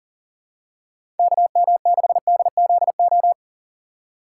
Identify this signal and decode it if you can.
Morse code, heard as a keyed tone, about 39 wpm, 700 Hz. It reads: KM6DZO
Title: USA Call Signs - 6 Characters - 6x Character Spacing - 40wpm